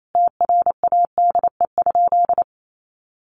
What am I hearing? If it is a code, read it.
Morse code, heard as a keyed tone, about 28 wpm, 700 Hz. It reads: TRADE?
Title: Sentences from Top 600 Words 28wpm